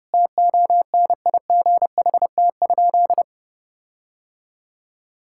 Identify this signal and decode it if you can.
Morse code, heard as a keyed tone, about 30 wpm, 700 Hz. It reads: TONIGHT?
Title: Sentences from Top 1600 Words 30wpm